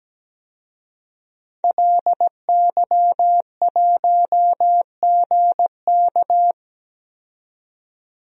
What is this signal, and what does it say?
Morse code, heard as a keyed tone, about 17 wpm, 700 Hz. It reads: LY1GK
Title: DXCC-100 Call Signs - 5 Characters - 17wpm